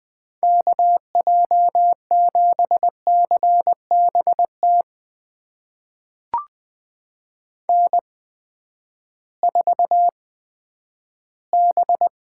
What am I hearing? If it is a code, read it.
Morse code, heard as a keyed tone, about 20 wpm, spaced as for 11 wpm, 700 Hz. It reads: KJ7CBT N4B
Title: USA Call Signs - 8x Character Spacing - 20wpm